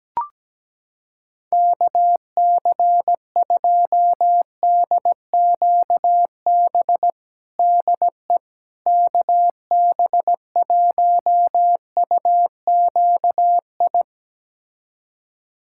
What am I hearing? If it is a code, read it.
Morse code, heard as a keyed tone, about 17 wpm, 700 Hz. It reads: KC2DQB DE KB1UQI
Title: USA Callsign DE Callsign 17wpm